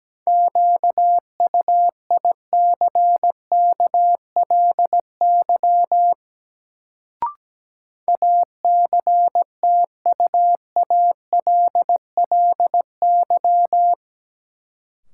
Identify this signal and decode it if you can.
Morse code, heard as a keyed tone, about 17 wpm, 700 Hz. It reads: QUICKLY ACTUALLY
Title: Sentences from Top 700 Words - New Word Drill - 17wpm